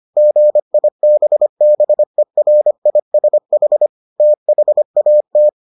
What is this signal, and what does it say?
Morse code, heard as a keyed tone, about 25 wpm, 600 Hz. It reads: GIBBERISH THAT